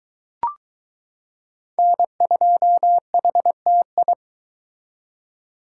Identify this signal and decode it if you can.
Morse code, heard as a keyed tone, about 21 wpm, 700 Hz. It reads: N2HTI